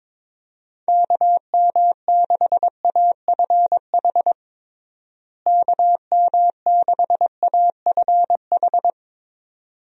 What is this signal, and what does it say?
Morse code, heard as a keyed tone, about 22 wpm, 700 Hz. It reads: KM6AFH KM6AFH
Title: USA Call Signs - 6 Characters - Repeated 2x - 22wpm